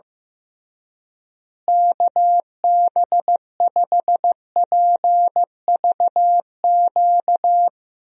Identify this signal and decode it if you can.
Morse code, heard as a keyed tone, about 15 wpm, 700 Hz. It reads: KB5PVQ
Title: USA Call Signs - 6 Characters - Repeated 3x - 15wpm